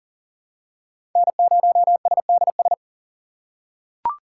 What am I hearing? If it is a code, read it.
Morse code, heard as a keyed tone, about 38 wpm, 700 Hz. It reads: N0SDS
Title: USA Call Signs - 3x Character Spacing - 40wpm